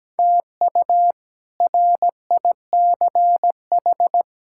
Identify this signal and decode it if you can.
Morse code, heard as a keyed tone, about 17 wpm, 700 Hz. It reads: TU RICH
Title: K1USN SST QSO Practice - 17wpm